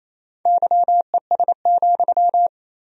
Morse code, heard as a keyed tone, about 28 wpm, 700 Hz. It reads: YES,
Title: 3x Word-Spacing - Sentences from Top 500 Words - 28wpm